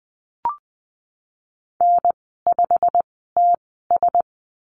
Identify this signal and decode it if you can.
Morse code, heard as a keyed tone, about 20 wpm, spaced as for 15 wpm, 700 Hz. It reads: N5TS
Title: 088 Lesson 15g - Number 5 - Call Signs 20wpm